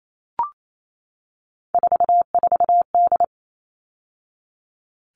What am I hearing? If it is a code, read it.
Morse code, heard as a keyed tone, about 28 wpm, 700 Hz. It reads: 44D
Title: All Cut Numbers - 3-Digit 28wpm